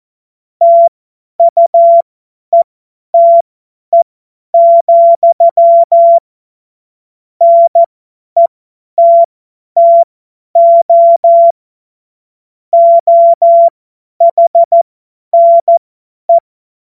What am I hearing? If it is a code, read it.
Morse code, heard as a keyed tone, about 14 wpm, spaced as for 10 wpm, 680 Hz. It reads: TUETE, NETTO OHNE
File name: Morsekurs / Witze / Witz 19. April 2022 50 F70